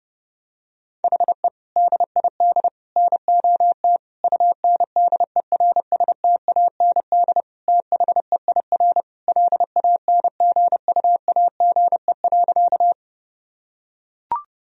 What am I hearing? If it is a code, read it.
Morse code, heard as a keyed tone, about 30 wpm, 700 Hz. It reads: HE DID NOT UNDERSTAND THEIR LANGUAGE.
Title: Sentences from Top 1000 Words - Review Part 3 - 30wpm